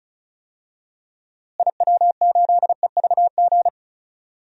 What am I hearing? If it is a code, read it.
Morse code, heard as a keyed tone, about 34 wpm, 700 Hz. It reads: IW8EVG
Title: DXCC-100 Call Signs - 6 Characters - Repeated 2x - 35wpm